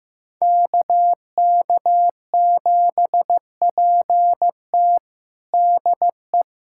Morse code, heard as a keyed tone, about 15 wpm, 700 Hz. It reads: KK7PT DE
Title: USA Callsign DE Callsign - 5 Characters - 15wpm